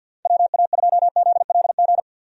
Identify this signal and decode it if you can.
Morse code, heard as a keyed tone, about 46 wpm, 700 Hz. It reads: WA2ZLG
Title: USA Call Signs - 6 Characters - Repeated 3x - 50wpm